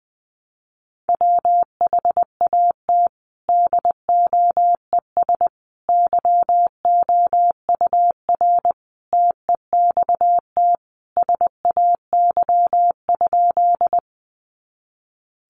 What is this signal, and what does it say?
Morse code, heard as a keyed tone, about 20 wpm, 700 Hz. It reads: WHAT DOES YOUR TEXT SAY?
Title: Sentences from Top 1200 Words 20wpm